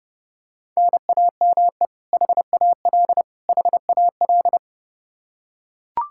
Morse code, heard as a keyed tone, about 30 wpm, 700 Hz. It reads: NAME HAL HAL
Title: Practice QSO - Exchange 1 - Beginning - 30wpm